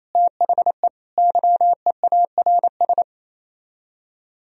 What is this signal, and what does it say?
Morse code, heard as a keyed tone, about 34 wpm, 700 Hz. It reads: THE YEARS
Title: Speed-Racing - Sets of 3 Words 28wpm